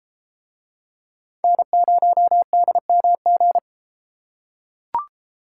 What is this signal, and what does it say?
Morse code, heard as a keyed tone, about 33 wpm, 700 Hz. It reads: N0DMG